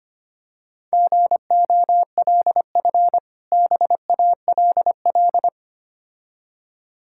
Morse code, heard as a keyed tone, about 25 wpm, 690 Hz. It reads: GOLF BALL
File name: practice / 25wpm / Difficult Words with Hints - 25wpm